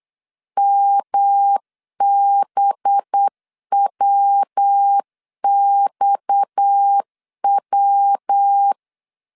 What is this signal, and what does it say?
Morse code, heard as a keyed tone, about 8 wpm, 800 Hz. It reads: MBWXW
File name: CW / DET / 1 6 1500C S A 8wpm 5203